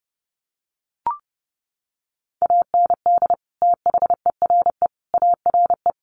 Morse code, heard as a keyed tone, about 30 wpm, 700 Hz. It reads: AND THERE ARE
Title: Sentences from Top 300 Words 30wpm